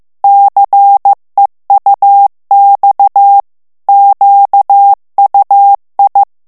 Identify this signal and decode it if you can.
Morse code, heard as a keyed tone, about 15 wpm, 800 Hz. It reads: CEUX QUI
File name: morse mp3 / FRANCAIS TEXTE CLAIR 2 VITESSE 14 MOTS MINUTES